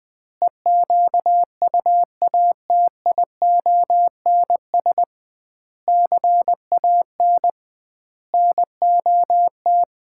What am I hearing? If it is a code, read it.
Morse code, heard as a keyed tone, about 20 wpm, 700 Hz. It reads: EQUATIONS CAN NOT
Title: 2x Word-Spacing - Sets of 5 Words - 20wpm